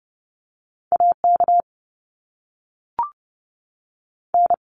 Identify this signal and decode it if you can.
Morse code, heard as a keyed tone, about 30 wpm, spaced as for 6 wpm, 700 Hz. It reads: AKN
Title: US States 30wpm